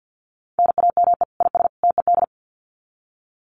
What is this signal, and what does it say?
Morse code, heard as a keyed tone, about 50 wpm, 700 Hz. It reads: NAME IS TED
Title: Common QSO Phrases 50wpm